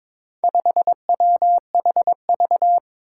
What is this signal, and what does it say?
Morse code, heard as a keyed tone, about 22 wpm, 700 Hz. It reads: 5WHV
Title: Four Challenging Letter-Number Combinations 22wpm